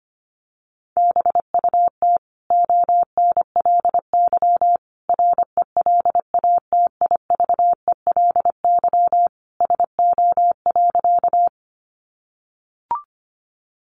Morse code, heard as a keyed tone, about 25 wpm, 700 Hz. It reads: BUT ONLY RELATIVELY SO.
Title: Sentences from Top 1100 Words 25wpm